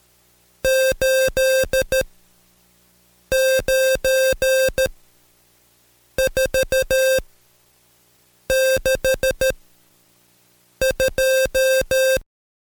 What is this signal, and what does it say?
Morse code, heard as a keyed tone, about 13 wpm, spaced as for 5 wpm, 530 Hz. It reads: 89462
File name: Code Quick / AudioDownld2 / Seg107-Day13-DayDrill 13